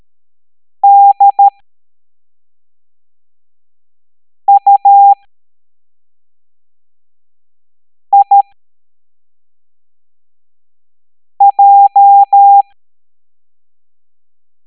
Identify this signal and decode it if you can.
Morse code, heard as a keyed tone, about 13 wpm, spaced as for 3 wpm, 800 Hz. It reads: DUIJ
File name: hamdata / L09-EX04